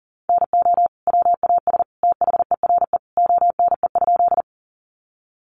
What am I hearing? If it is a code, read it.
Morse code, heard as a keyed tone, about 40 wpm, 700 Hz. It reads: NO WAS THERE ONE?
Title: Sentences from Top 100 Words 40wpm